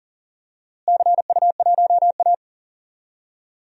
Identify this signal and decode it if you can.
Morse code, heard as a keyed tone, about 40 wpm, 700 Hz. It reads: CU1A
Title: DXCC-100 Call Signs - 4 Characters - Repeated 3x - 40wpm